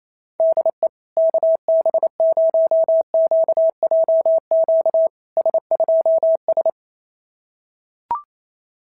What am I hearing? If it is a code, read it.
Morse code, heard as a keyed tone, about 28 wpm, 650 Hz. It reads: DE KB0QJQ S2S
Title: SOTA QSO Practice 28wpm